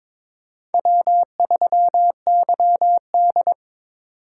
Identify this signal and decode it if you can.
Morse code, heard as a keyed tone, about 26 wpm, 700 Hz. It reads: W3YD
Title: Speed Racing - USA Call Signs 22wpm